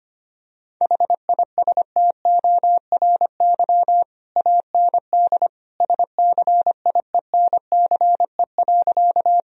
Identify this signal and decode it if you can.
Morse code, heard as a keyed tone, about 25 wpm, 700 Hz. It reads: HISTORY AND SCIENCE.
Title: Sentences from Top 1000 Words - Review Part 2 - 25wpm